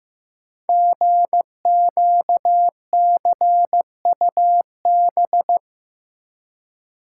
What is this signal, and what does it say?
Morse code, heard as a keyed tone, about 15 wpm, 700 Hz. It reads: GQCUB
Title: Code Groups - 5 Letter:Numbers - 15wpm